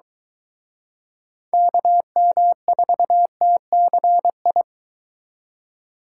Morse code, heard as a keyed tone, about 24 wpm, 700 Hz. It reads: KM4TCI